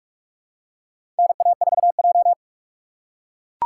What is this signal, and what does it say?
Morse code, heard as a keyed tone, about 44 wpm, 690 Hz. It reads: NA4J
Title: USA Call Signs - 4 Characters - 2x Character Spacing - 45wpm